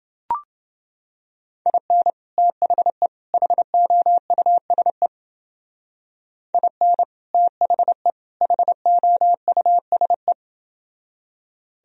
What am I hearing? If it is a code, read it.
Morse code, heard as a keyed tone, about 30 wpm, 700 Hz. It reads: IN THE HOUSE IN THE HOUSE